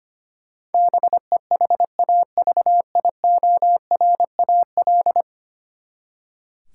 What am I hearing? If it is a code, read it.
Morse code, heard as a keyed tone, about 25 wpm, 700 Hz. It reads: BEHAVIORAL
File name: Ten Letter Words - 25wpm